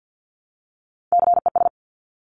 Morse code, heard as a keyed tone, about 50 wpm, 700 Hz. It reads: CES
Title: Top Three Letter Combinations in Words 50wpm